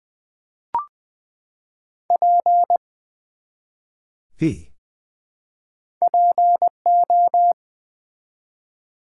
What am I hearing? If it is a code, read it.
Morse code, heard as a keyed tone, about 20 wpm, 700 Hz. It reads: P PO
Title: Build A Word - 1 Letter At A Time - 3 to 5-Letter Words - 20wpm